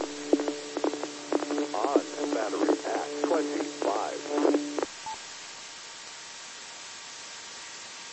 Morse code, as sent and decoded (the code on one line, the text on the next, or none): none